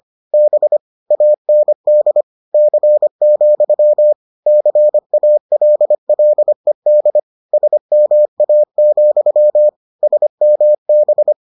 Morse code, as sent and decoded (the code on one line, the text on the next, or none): -... / .- -. -.. / -.-. --..-- / -.-. .- .-.. .-.. -.. / ... -- .- --..-- / ... -- -...
B AND C, CALLED SMA, SMB